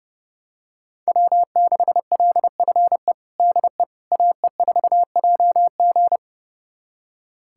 Morse code, heard as a keyed tone, about 30 wpm, 700 Hz. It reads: W6LFE DE AE4JG
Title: USA Callsign DE Callsign - 5 Characters - 30wpm